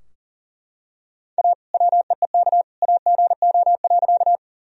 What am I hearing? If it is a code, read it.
Morse code, heard as a keyed tone, about 40 wpm, 700 Hz. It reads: A WEEK AGO.
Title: Sentences from Top 1000 Words - Review Part 3 - 40wpm